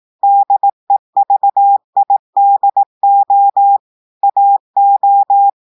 Morse code, heard as a keyed tone, about 18 wpm, 800 Hz. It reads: DEVIDO AO
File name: audios / portugais / portugais texte en clair 1 vitesse 18 mots minutes